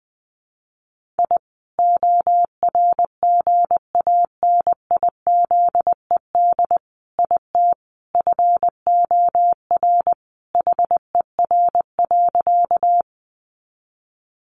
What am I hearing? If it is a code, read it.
Morse code, heard as a keyed tone, about 20 wpm, 700 Hz. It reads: I ORGANIZED IT FOR HER.
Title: Sentences from Top 1300 Words 20wpm